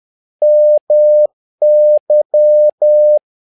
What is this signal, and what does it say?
Morse code, heard as a keyed tone, about 10 wpm, 600 Hz. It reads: MY
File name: learning center / tower trouble / tower trouble 10wpm